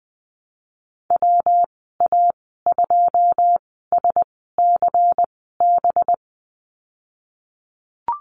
Morse code, heard as a keyed tone, about 20 wpm, spaced as for 15 wpm, 700 Hz. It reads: WA2SCB
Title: 200 Lesson 32g - Letter k - Call Signs 20wpm